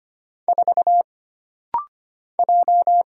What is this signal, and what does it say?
Morse code, heard as a keyed tone, about 25 wpm, 700 Hz. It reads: 4 J
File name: Single Character (Letter:Number:Punctuation:Prosign) - Rapid-Fire - 25wpm